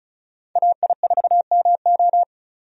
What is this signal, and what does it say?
Morse code, heard as a keyed tone, about 34 wpm, 700 Hz. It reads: AI4MO
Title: USA Call Signs - 5 Characters - 35wpm